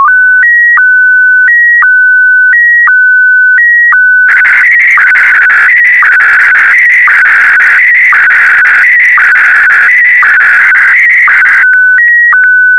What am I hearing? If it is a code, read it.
SSTV (slow-scan television), Scottie DX.
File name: sstv / SDX-VK4MSL